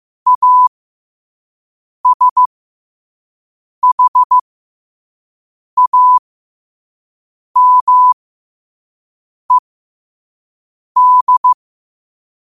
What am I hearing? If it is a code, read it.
Morse code, heard as a keyed tone, about 15 wpm, spaced as for 5 wpm, 1000 Hz. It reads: ASHAMED
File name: 5wpm